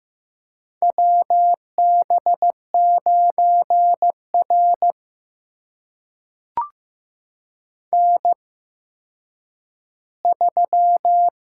Morse code, heard as a keyed tone, about 15 wpm, spaced as for 9 wpm, 700 Hz. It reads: WB9R N3
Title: USA Call Signs - 8x Character Spacing - 15wpm